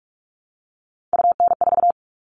Morse code, heard as a keyed tone, about 44 wpm, 700 Hz. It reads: UN4
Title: All Cut Numbers - 3-Digit 45wpm